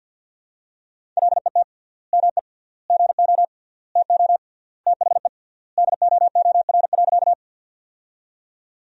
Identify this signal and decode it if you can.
Morse code, heard as a keyed tone, about 50 wpm, 700 Hz. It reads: LET ME GO TO THE DOOR.
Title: 3x Word-Spacing - Sentences from Top 300 Words - 50wpm